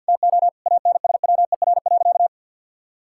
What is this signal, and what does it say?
Morse code, heard as a keyed tone, about 50 wpm, 700 Hz. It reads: TO ANSWER.
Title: Sentences from Top 1600 Words 50wpm